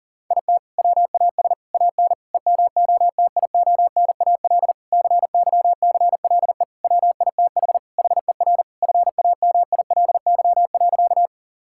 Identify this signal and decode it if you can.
Morse code, heard as a keyed tone, about 40 wpm, 700 Hz. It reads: IT WAS AN EMOTIONAL CYCLE WITH HER FAMILY.
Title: Sentences from Top 1000 Words - Review Part 6 - 40wpm